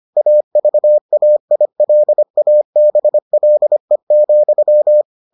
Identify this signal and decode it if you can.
Morse code, heard as a keyed tone, about 25 wpm, 600 Hz. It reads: AVAILABLE,